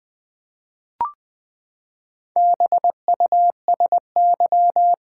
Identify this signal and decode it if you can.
Morse code, heard as a keyed tone, about 20 wpm, 700 Hz. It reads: BUSY
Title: Four Letter Words - 20wpm